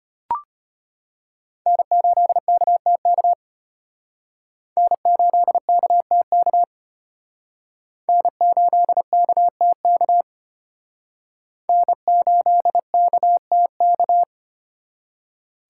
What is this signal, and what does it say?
Morse code, heard as a keyed tone, about 30 wpm, 700 Hz. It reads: N8KTK N8KTK N8KTK N8KTK
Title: Speed Racing - USA Call Signs 25wpm